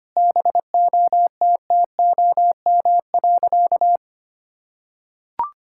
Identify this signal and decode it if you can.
Morse code, heard as a keyed tone, about 25 wpm, 700 Hz. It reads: BOTTOM.
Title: Sentences from Top 1000 Words - Review Part 6 - 25wpm